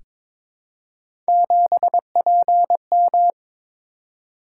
Morse code, heard as a keyed tone, about 22 wpm, 700 Hz. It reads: 7PM
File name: practice / 22wpm / Three Letter-Number Combinations - 22wpm